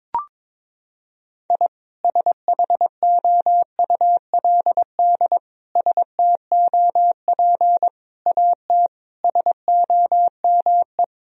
Morse code, heard as a keyed tone, about 22 wpm, 700 Hz. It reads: I SHOULD STOP AT SOME
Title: Sentences from Top 1000 Words - Review Part 1 - 22wpm